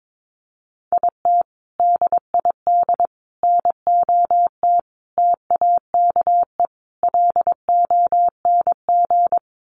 Morse code, heard as a keyed tone, about 22 wpm, 700 Hz. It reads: IT DID NOT TAKE LONG
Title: Sets of 5 Words Encore2 22wpm